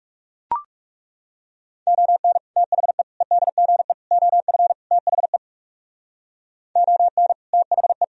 Morse code, heard as a keyed tone, about 38 wpm, 700 Hz. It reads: ON THE EDGE OF THE ON THE